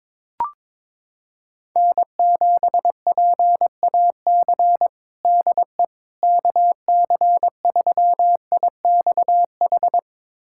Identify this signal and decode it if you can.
Morse code, heard as a keyed tone, about 22 wpm, 700 Hz. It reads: N7PAC DE KC3IXH
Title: USA Callsign DE Callsign 22wpm